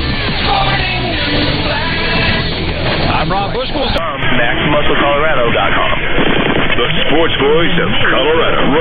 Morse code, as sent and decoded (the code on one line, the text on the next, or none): none